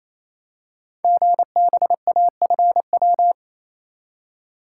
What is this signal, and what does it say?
Morse code, heard as a keyed tone, about 29 wpm, 700 Hz. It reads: GBAFW
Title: Code Groups - 5 Letters - 28wpm